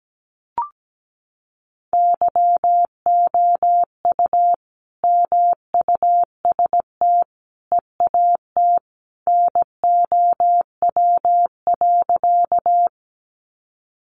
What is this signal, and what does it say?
Morse code, heard as a keyed tone, about 17 wpm, 700 Hz. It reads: YOU MUST EAT NOW.
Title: Sentences from Top 300 Words 17wpm